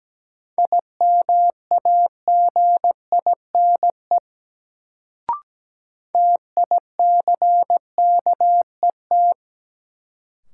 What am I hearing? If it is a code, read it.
Morse code, heard as a keyed tone, about 17 wpm, 700 Hz. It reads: IMAGINE TICKET
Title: Sentences from Top 1400 Words - New Word Drill - 17wpm